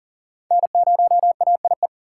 Morse code, heard as a keyed tone, about 40 wpm, 700 Hz. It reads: N0AIE
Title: K1USN SST QSO Practice - 40wpm